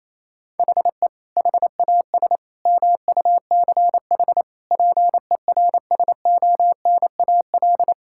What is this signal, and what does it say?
Morse code, heard as a keyed tone, about 27 wpm, 700 Hz. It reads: HE HAS MUCH PERSONAL
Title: Sentences from Top 1000 Words - Review Part 5 - 28wpm